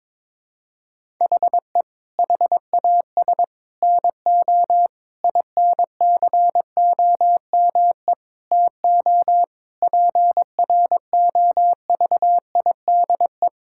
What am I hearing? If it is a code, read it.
Morse code, heard as a keyed tone, about 22 wpm, 700 Hz. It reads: HE HAS NO INCOME TO PROVIDE
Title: Sentences from Top 1000 Words - Review Part 5 - 22wpm